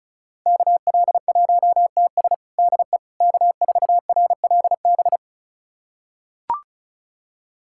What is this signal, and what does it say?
Morse code, heard as a keyed tone, about 36 wpm, 700 Hz. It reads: KL1TS DE K4RLB
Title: USA Callsign DE Callsign 35wpm